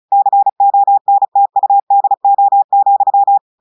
Morse code, heard as a keyed tone, about 35 wpm, 800 Hz. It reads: CONTUDO,
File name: audios / portugais / portugais texte en clair 2 vitesse 35 mots minutes